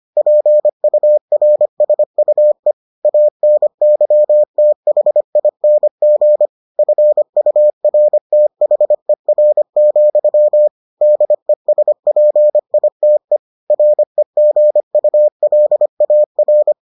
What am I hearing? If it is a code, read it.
Morse code, heard as a keyed tone, about 25 wpm, 600 Hz. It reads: PURSUE ANYTHING FURTHER, DESPITE REGULAR